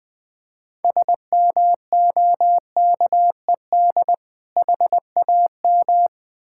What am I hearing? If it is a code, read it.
Morse code, heard as a keyed tone, about 20 wpm, 700 Hz. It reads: SMOKED HAM
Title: Difficult Words with Hints - 20wpm